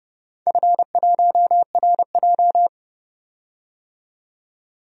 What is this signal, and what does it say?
Morse code, heard as a keyed tone, about 30 wpm, 700 Hz. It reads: F1RJ
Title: DXCC-100 Call Signs - 4 Characters - 30wpm